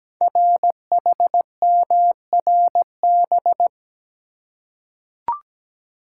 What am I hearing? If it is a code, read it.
Morse code, heard as a keyed tone, about 17 wpm, 700 Hz. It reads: RHMRB
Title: Code Groups - 5 Letters - 17wpm